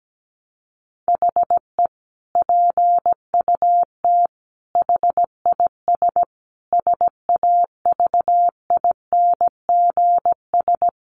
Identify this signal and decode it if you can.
Morse code, heard as a keyed tone, about 17 wpm, 700 Hz. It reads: HE PUT HIS SAVINGS
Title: Sentences from Top 1200 Words 17wpm